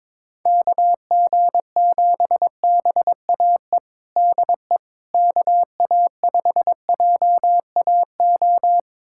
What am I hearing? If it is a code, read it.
Morse code, heard as a keyed tone, about 22 wpm, 700 Hz. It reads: KG7BAE DE KA5JAO
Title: USA Callsign DE Callsign - 6 Characters - 22wpm